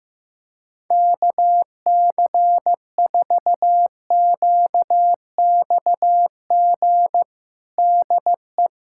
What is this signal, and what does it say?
Morse code, heard as a keyed tone, about 15 wpm, 700 Hz. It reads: KC4QXG DE
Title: USA Callsign DE Callsign - 6 Characters - 15wpm